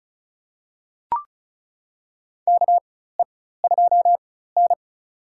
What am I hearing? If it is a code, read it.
Morse code, heard as a keyed tone, about 36 wpm, spaced as for 16 wpm, 700 Hz. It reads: KE2N
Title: USA Call Signs - 4 Characters - 4x Character Spacing - 35wpm